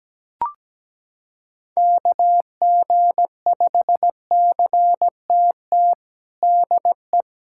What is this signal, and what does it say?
Morse code, heard as a keyed tone, about 17 wpm, 700 Hz. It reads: KG5CTT DE
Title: USA Callsign DE Callsign 17wpm